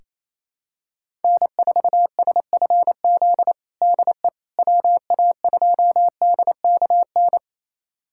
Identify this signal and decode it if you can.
Morse code, heard as a keyed tone, about 28 wpm, 700 Hz. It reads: N4SFZ DE WA2DKN
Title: USA Callsign DE Callsign 28wpm